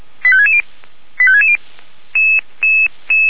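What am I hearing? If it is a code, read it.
Five-tone selective call (ZVEI), heard as quick runs of five tones, tones of about 70 ms. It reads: 85660 85660
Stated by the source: ZVEI five-tone selective call, code 85861, received by SDR.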